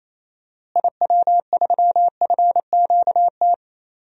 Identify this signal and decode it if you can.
Morse code, heard as a keyed tone, about 27 wpm, 700 Hz. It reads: IW3FQT
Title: DXCC-100 Call Signs - 6 Characters - Repeated 3x - 28wpm